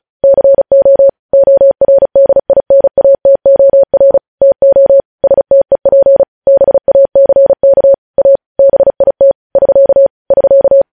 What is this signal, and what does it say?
Morse code, heard as a keyed tone, about 36 wpm, 550 Hz. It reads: CO ORDINATOR TO STEP BACK A BIT <SK> <SK>
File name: podcast / News headlines-1x-35x35